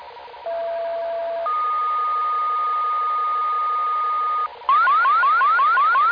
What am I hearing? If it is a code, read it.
Two-tone page, heard as a one-second tone, then a three-second tone, about 670 then 1230 Hz. It is heard from 0.5 to 4.5 s.